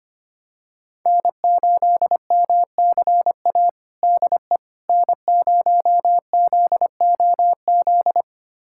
Morse code, heard as a keyed tone, about 25 wpm, 700 Hz. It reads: N8MCA DE N0ZOZ
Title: USA Callsign DE Callsign - 5 Characters - 25wpm